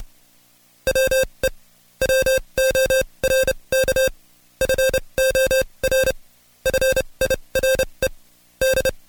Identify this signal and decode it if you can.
Morse code, heard as a keyed tone, about 29 wpm, spaced as for 23 wpm, 530 Hz. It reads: WE WORK FOR FIRE D
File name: Code Quick / AudioDownld2 / Seg138-Stepper11 23WPM